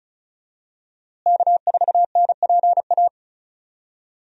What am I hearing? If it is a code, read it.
Morse code, heard as a keyed tone, about 36 wpm, 700 Hz. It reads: K4NPA
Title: USA Call Signs - 5 Characters - Repeated 3x - 35wpm